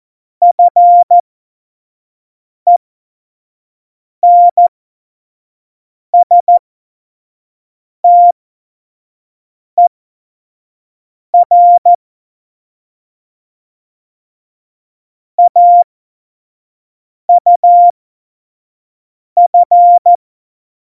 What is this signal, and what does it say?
Morse code, heard as a keyed tone, about 14 wpm, spaced as for 5 wpm, 700 Hz. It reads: FENSTER AUF
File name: Morsekurs / Witze / Witz 17.Juli 2018, 25 F70